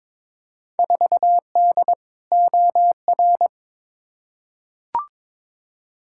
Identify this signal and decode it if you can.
Morse code, heard as a keyed tone, about 22 wpm, 700 Hz. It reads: 4D OR
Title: Field Day Exchanges Encore - 22wpm